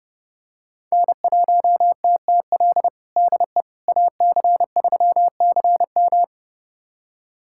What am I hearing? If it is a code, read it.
Morse code, heard as a keyed tone, about 32 wpm, 700 Hz. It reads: N1TTL DE AC3CM
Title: USA Callsign DE Callsign - 5 Characters - 30wpm